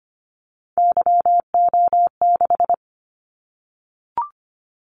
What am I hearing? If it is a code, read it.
Morse code, heard as a keyed tone, about 25 wpm, 700 Hz. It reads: YO6
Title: Callsign Prefixes for Top 25 DXCC - 25wpm